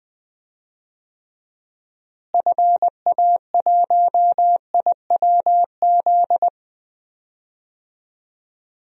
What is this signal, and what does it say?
Morse code, heard as a keyed tone, about 20 wpm, 700 Hz. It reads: FA1IWZ
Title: DXCC-100 Call Signs - 6 Characters - Repeated 3x - 20wpm